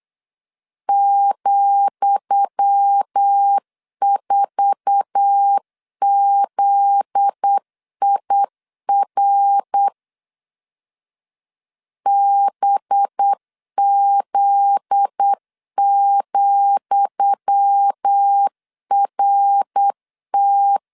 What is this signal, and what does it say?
Morse code, heard as a keyed tone, about 8 wpm, 800 Hz. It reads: ,4ZIR BZ,RT
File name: LEZIONI / 10 SIGNS / 10D COMA 8wpm 3749